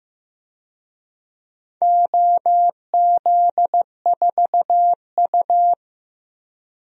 Morse code, heard as a keyed tone, about 15 wpm, 700 Hz. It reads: OZ4U
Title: DXCC-100 Call Signs - 4 Characters - 15wpm